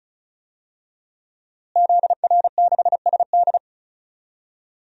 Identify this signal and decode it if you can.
Morse code, heard as a keyed tone, about 34 wpm, 700 Hz. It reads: ZR6SD
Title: DXCC-100 Call Signs - 5 Characters - Repeated 3x - 35wpm